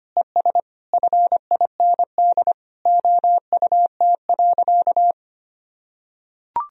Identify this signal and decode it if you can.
Morse code, heard as a keyed tone, about 25 wpm, 700 Hz. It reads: ES FIND OUT.
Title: Sentences from Top 200 Words and Abbreviations 25wpm